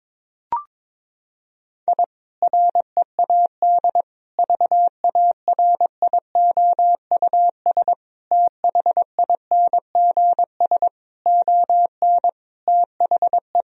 Morse code, heard as a keyed tone, about 22 wpm, 700 Hz. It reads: I READ VARIOUS THINGS ON THE
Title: Sentences from Top 1000 Words - Review Part 4 - 22wpm